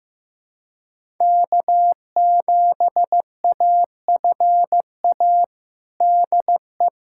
Morse code, heard as a keyed tone, about 15 wpm, 700 Hz. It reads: K7AFA DE
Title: USA Callsign DE Callsign - 5 Characters - 15wpm